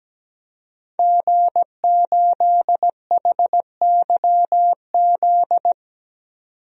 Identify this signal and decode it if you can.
Morse code, heard as a keyed tone, about 17 wpm, 700 Hz. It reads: G8HYZ
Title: DXCC-100 Call Signs - 5 Characters - 17wpm